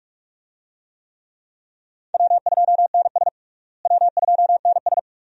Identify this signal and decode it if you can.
Morse code, heard as a keyed tone, about 42 wpm, 700 Hz. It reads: W2NS W2NS
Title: USA Call Signs - 4 Characters - Repeated 2x - 45wpm